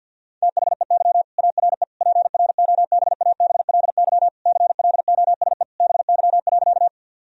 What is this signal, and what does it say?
Morse code, heard as a keyed tone, about 50 wpm, 700 Hz. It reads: THEY ARE PROBABLY CLOSE BY.
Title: Sentences from Top 1000 Words - Review Part 4 - 50wpm